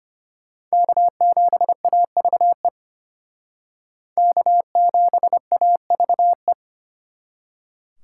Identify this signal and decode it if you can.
Morse code, heard as a keyed tone, about 33 wpm, 700 Hz. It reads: K7AVE K7AVE